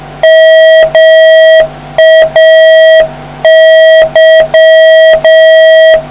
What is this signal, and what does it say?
Morse code, heard as a keyed tone, about 6 wpm, 1950 Hz, with noise. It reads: MAY